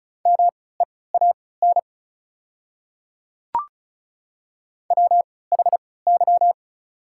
Morse code, heard as a keyed tone, about 34 wpm, spaced as for 20 wpm, 700 Hz. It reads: MEAN WHY
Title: 3x Character Spacing - Top 200 Words - 35wpm